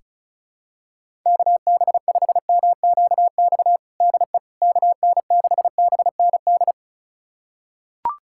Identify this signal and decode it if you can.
Morse code, heard as a keyed tone, about 34 wpm, 700 Hz. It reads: KB5MQX DE KN6BND
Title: USA Callsign DE Callsign 35wpm